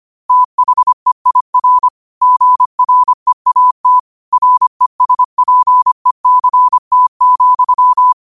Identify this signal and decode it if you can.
Morse code, heard as a keyed tone, about 25 wpm, 1000 Hz. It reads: THEIR GREAT RESPECT,
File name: qotdfiles / 25wpm